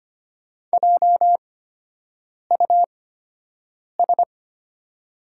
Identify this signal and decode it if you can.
Morse code, heard as a keyed tone, about 25 wpm, spaced as for 7 wpm, 690 Hz. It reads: JUS
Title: 8x Character Spacing - Top 500 Words - 25wpm